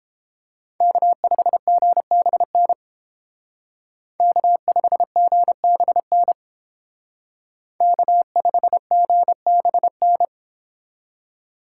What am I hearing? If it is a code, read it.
Morse code, heard as a keyed tone, about 28 wpm, 700 Hz. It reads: K5GBN K5GBN K5GBN